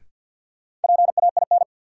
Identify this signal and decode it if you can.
Morse code, heard as a keyed tone, about 48 wpm, 700 Hz. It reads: PAIN
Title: Sentences from Top 1600 Words - New Word Drill - 50wpm